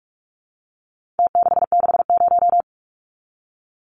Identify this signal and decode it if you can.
Morse code, heard as a keyed tone, about 44 wpm, 700 Hz. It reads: T660
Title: Numbers - Contest Style - 4-Digit 45wpm